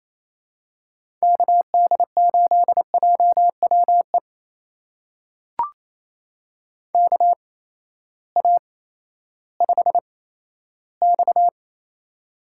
Morse code, heard as a keyed tone, about 28 wpm, spaced as for 14 wpm, 700 Hz. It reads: KD8JWE KA5X
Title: USA Call Signs - 8x Character Spacing - 28wpm